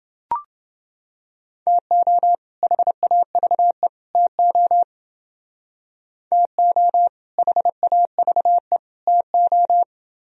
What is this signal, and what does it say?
Morse code, heard as a keyed tone, about 29 wpm, 700 Hz. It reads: TO HAVE TO TO HAVE TO